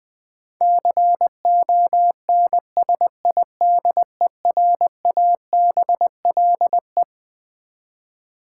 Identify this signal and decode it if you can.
Morse code, heard as a keyed tone, about 20 wpm, 700 Hz. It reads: CONSIDERABLE
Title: Twelve Letter Words - 20wpm